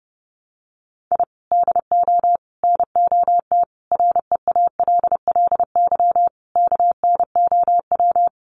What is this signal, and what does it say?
Morse code, heard as a keyed tone, about 30 wpm, 700 Hz. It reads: I DO NOT REALLY KNOW
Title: Sets of 5 Words Encore 30wpm